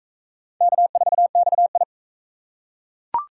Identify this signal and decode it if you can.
Morse code, heard as a keyed tone, about 38 wpm, 700 Hz. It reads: K4XI